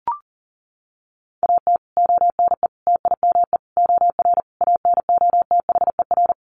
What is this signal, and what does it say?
Morse code, heard as a keyed tone, about 40 wpm, 700 Hz. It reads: AT ONE TIME OR ANOTHER
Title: Sets of 5 Words Encore 40wpm